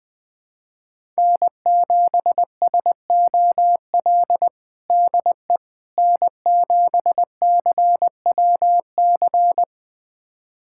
Morse code, heard as a keyed tone, about 20 wpm, 700 Hz. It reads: N7SOL DE N7CWC
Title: USA Callsign DE Callsign - 5 Characters - 20wpm